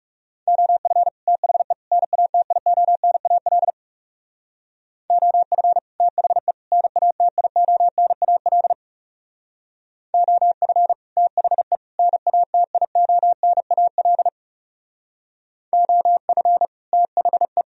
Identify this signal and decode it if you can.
Morse code, heard as a keyed tone, about 40 wpm, 700 Hz. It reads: OF THE NATIONAL OF THE NATIONAL OF THE NATIONAL OF THE